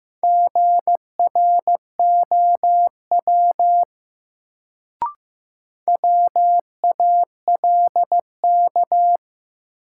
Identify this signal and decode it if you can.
Morse code, heard as a keyed tone, about 15 wpm, 700 Hz. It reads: GROW WALK
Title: Sentences from Top 300 Words - New Word Drill - 15wpm